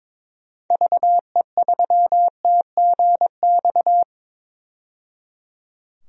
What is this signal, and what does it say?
Morse code, heard as a keyed tone, about 22 wpm, 700 Hz. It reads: VE3TGX
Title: DXCC-100 Call Signs - 6 Characters - 22wpm